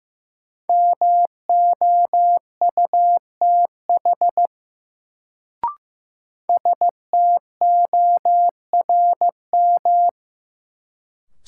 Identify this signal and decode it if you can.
Morse code, heard as a keyed tone, about 15 wpm, 700 Hz. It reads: MOUTH STORM
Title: Sentences from Top 1500 Words - New Word Drill Alt - 15wpm